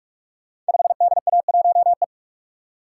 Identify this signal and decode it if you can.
Morse code, heard as a keyed tone, about 44 wpm, 700 Hz. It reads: 5DA1E
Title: Code Groups - 5 Letter:Numbers - 45wpm